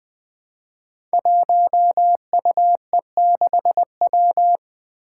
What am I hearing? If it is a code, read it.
Morse code, heard as a keyed tone, about 20 wpm, 700 Hz. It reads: 1UE6W
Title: Five Challenging Letter-Number Combinations 20wpm